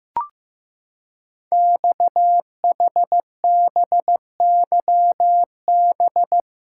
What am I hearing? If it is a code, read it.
Morse code, heard as a keyed tone, about 15 wpm, 700 Hz. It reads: XHBYB
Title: Code Groups - 5 Letters - 15wpm